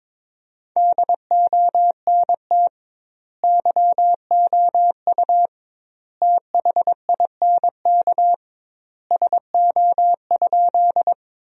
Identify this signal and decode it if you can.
Morse code, heard as a keyed tone, about 22 wpm, 700 Hz. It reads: DONT YOU THINK SO?
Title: 2x Word-Spacing - Sentences from Top 300 Words - 22wpm